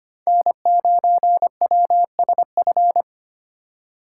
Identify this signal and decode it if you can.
Morse code, heard as a keyed tone, about 25 wpm, 700 Hz. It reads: N9WSF
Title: USA Call Signs - 5 Characters - 2x Character Spacing - 25wpm